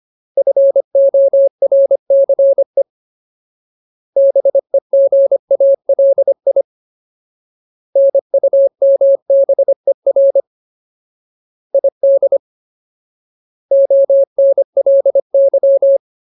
Morse code, heard as a keyed tone, about 25 wpm, 550 Hz. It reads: FORCE BEGALI NUMBER ID ONLY
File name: Ultimate Word List / 25WPM Ultimate